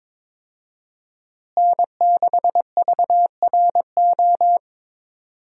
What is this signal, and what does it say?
Morse code, heard as a keyed tone, about 22 wpm, 700 Hz. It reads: N6VRO
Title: USA Call Signs - 5 Characters - Repeated 3x - 22wpm